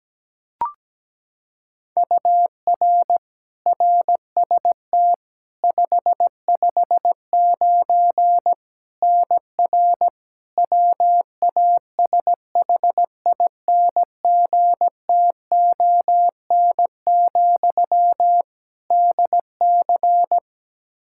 Morse code, heard as a keyed tone, about 17 wpm, 700 Hz. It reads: UR RST 559 NR WASHINGTON, DC
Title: K1USN SST QSO Practice 17wpm